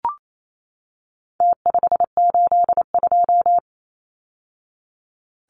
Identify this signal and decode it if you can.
Morse code, heard as a keyed tone, about 28 wpm, 700 Hz. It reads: T582
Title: Numbers - Contest Style - 4-Digit 28wpm